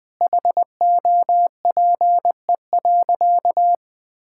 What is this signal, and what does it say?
Morse code, heard as a keyed tone, about 21 wpm, 700 Hz. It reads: HOPE.
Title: Sentences from Top 1000 Words - Review Part 6 - 20wpm